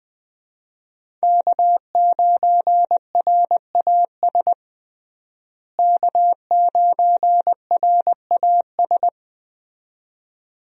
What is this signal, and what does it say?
Morse code, heard as a keyed tone, about 21 wpm, 700 Hz. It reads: K9RAS K9RAS
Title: USA Call Signs - 5 Characters - Repeated 2x - 20wpm